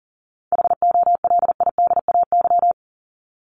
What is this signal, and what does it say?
Morse code, heard as a keyed tone, about 40 wpm, 700 Hz. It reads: HOLIDAY
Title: Top 1501-1750 Words 40wpm